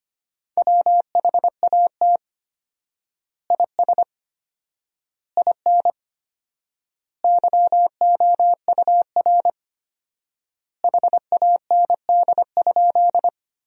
Morse code, heard as a keyed tone, about 25 wpm, 700 Hz. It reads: WHAT IS IN YOUR HAND?
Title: 4x Word-Spacing - Sentences from Top 200 Words - 25wpm